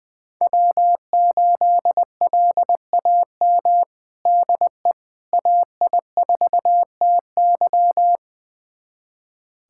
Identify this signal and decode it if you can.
Morse code, heard as a keyed tone, about 20 wpm, 700 Hz. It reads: W8LAM DE AI4TY
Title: USA Callsign DE Callsign - 5 Characters - 20wpm